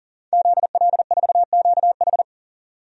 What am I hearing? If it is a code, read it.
Morse code, heard as a keyed tone, about 40 wpm, 700 Hz. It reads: ZL4QH
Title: DXCC-100 Call Signs - 5 Characters - 40wpm